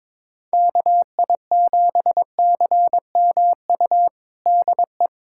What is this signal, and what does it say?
Morse code, heard as a keyed tone, about 21 wpm, 700 Hz. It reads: KI7CMU DE
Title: USA Callsign DE Callsign 22wpm